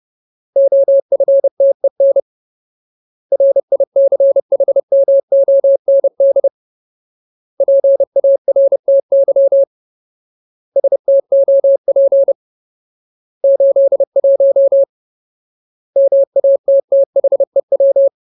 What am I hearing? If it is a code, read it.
Morse code, heard as a keyed tone, about 30 wpm, 550 Hz. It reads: OFTEN RICHMOND PARTY STOP 81 MATTHEW
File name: Ultimate Word List / 30WPM Ultimate